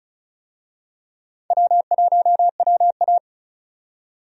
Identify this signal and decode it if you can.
Morse code, heard as a keyed tone, about 36 wpm, 700 Hz. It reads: W1WA
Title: USA Call Signs - 4 Characters - 35wpm